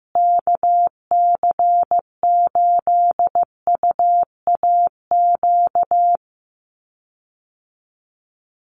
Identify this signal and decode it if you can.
Morse code, heard as a keyed tone, about 15 wpm, 700 Hz. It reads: KC8UAQ
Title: USA Call Signs Repeated 3x 15wpm